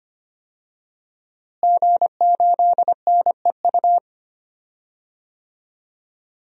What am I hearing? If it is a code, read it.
Morse code, heard as a keyed tone, about 25 wpm, 700 Hz. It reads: G8NEU
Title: DXCC-100 Call Signs - 5 Characters - Repeated 2x - 25wpm